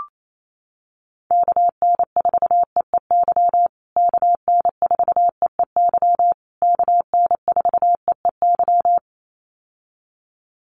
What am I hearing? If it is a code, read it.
Morse code, heard as a keyed tone, about 28 wpm, 700 Hz. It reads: KN4EEY KN4EEY KN4EEY
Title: USA Call Signs Repeated 3x 28wpm